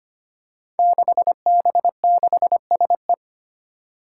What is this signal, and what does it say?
Morse code, heard as a keyed tone, about 25 wpm, 700 Hz. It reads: 6B6SE
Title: Five Challenging Letter-Number Combinations 25wpm